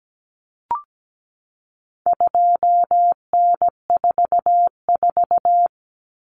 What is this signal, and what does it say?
Morse code, heard as a keyed tone, about 17 wpm, 700 Hz. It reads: 2N44
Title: Common Cut Numbers - 4-Digit 17wpm